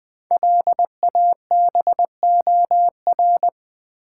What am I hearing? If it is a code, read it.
Morse code, heard as a keyed tone, about 20 wpm, 700 Hz. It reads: LABOR
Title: Build A Word - 5 Letters At A Time - 10-Letter Words - 20wpm